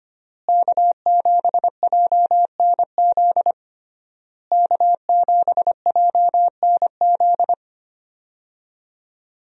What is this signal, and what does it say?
Morse code, heard as a keyed tone, about 25 wpm, 700 Hz. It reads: K7JNZ K7JNZ
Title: USA Call Signs - 5 Characters - Repeated 3x - 25wpm